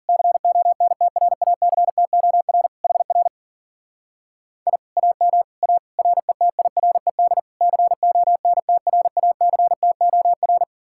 Morse code, heard as a keyed tone, about 48 wpm, 700 Hz. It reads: CONTRACTOR HR I AM A RETIRED CONTRACTOR
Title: Speed-Racing Common QSO Phrases Encore - 40wpm